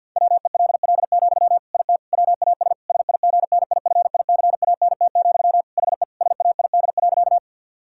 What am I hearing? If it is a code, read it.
Morse code, heard as a keyed tone, about 50 wpm, 700 Hz. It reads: WELL, IT WAS SIGNIFICANT, HE SAID.
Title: Sentences from Top 1000 Words - Review Part 6 - 50wpm